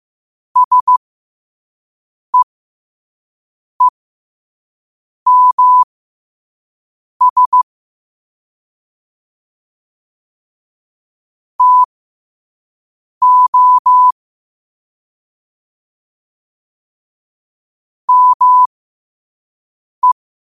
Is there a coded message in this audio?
Yes: Morse code from 0.6 to 20.1 s, heard as a keyed tone, about 15 wpm, spaced as for 5 wpm, 1000 Hz. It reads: SEEMS TO ME